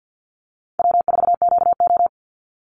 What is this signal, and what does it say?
Morse code, heard as a keyed tone, about 48 wpm, 700 Hz. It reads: W4QO
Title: Call Signs Repeated 3x - 50wpm